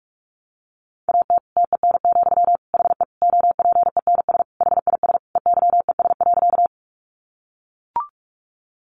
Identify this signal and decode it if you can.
Morse code, heard as a keyed tone, about 44 wpm, 700 Hz. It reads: AT TEN, HE OPENS HIS EYES.
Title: Easy Phrases 45wpm